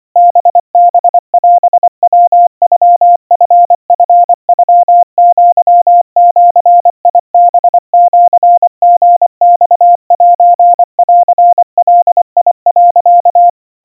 Morse code (Japanese Wabun code), heard as a keyed tone, about 24 wpm, 700 Hz. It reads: ハハオヤノチチノアジハシリマセンガ、
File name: 90 cpm 700 Hz Crispy